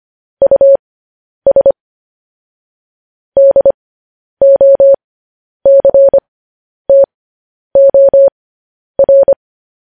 Morse code, heard as a keyed tone, about 25 wpm, spaced as for 10 wpm, 550 Hz. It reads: US DOCTOR